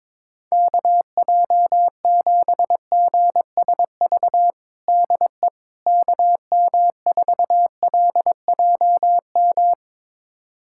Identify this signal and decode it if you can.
Morse code, heard as a keyed tone, about 22 wpm, 700 Hz. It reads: KJ7GSV DE KM4LJM
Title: USA Callsign DE Callsign - 6 Characters - 22wpm